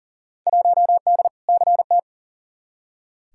Morse code, heard as a keyed tone, about 38 wpm, 700 Hz. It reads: 1D CT
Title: Field Day Exchanges - 40wpm